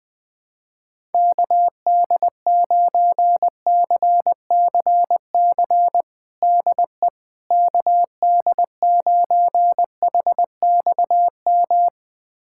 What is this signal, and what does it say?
Morse code, heard as a keyed tone, about 21 wpm, 700 Hz. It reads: KD9CCC DE KD9HXM
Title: USA Callsign DE Callsign - 6 Characters - 20wpm